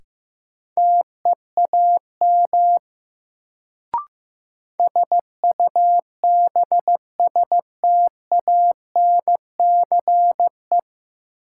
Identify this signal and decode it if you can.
Morse code, heard as a keyed tone, about 15 wpm, 700 Hz. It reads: TEAM SUBSTANCE
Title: Sentences from Top 1500 Words - New Word Drill Alt - 15wpm